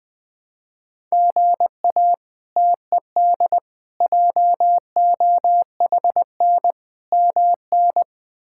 Morse code, heard as a keyed tone, about 20 wpm, 700 Hz. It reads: GA TED JOHN MN
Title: K1USN SST QSO Practice - 20wpm